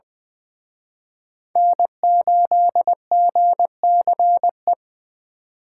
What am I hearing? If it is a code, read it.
Morse code, heard as a keyed tone, about 23 wpm, 700 Hz. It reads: N8GCE
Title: Speed Racing - USA 1x3 Call Signs 20wpm